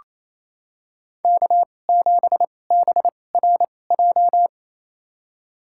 Morse code, heard as a keyed tone, about 28 wpm, spaced as for 20 wpm, 700 Hz. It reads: K7BRJ
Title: USA Call Signs - 5 Characters - 2x Character Spacing - 28wpm